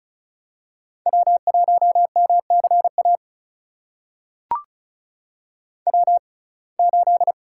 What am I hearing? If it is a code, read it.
Morse code, heard as a keyed tone, about 36 wpm, spaced as for 25 wpm, 700 Hz. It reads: W1MCA W8
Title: USA Call Signs - 6x Character Spacing - 35wpm